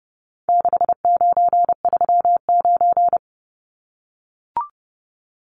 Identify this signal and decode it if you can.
Morse code, heard as a keyed tone, about 30 wpm, 700 Hz. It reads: 6939
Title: Numbers - 4-Digit 30wpm